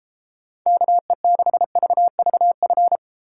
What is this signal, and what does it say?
Morse code, heard as a keyed tone, about 32 wpm, 700 Hz. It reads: KE6VVF